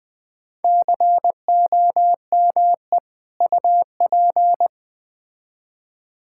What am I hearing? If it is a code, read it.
Morse code, heard as a keyed tone, about 21 wpm, 700 Hz. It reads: COME UP
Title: Build A Phrase - 3-Word Phrases - 20wpm